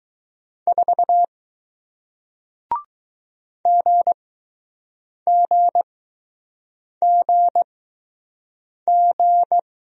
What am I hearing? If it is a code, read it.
Morse code, heard as a keyed tone, about 20 wpm, spaced as for 6 wpm, 700 Hz. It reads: 4 GGGG